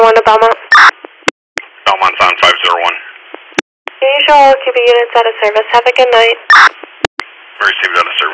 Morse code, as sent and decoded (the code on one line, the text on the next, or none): none